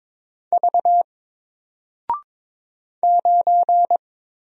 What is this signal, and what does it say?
Morse code, heard as a keyed tone, about 22 wpm, 700 Hz. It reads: V 9
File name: Single-Letter-Number-22wpm